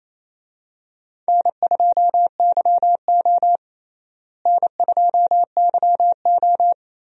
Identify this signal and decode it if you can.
Morse code, heard as a keyed tone, about 28 wpm, 700 Hz. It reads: N2YO N2YO
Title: USA Call Signs - 4 Characters - Repeated 3x - 28wpm